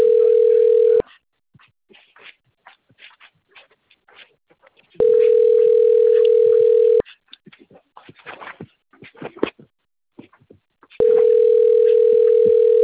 Ringback tone (North America), 440 + 480 Hz.